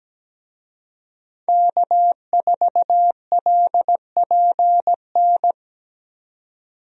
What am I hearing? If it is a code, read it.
Morse code, heard as a keyed tone, about 17 wpm, 700 Hz. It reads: K4LPN
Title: USA Call Signs - 5 Characters - 17wpm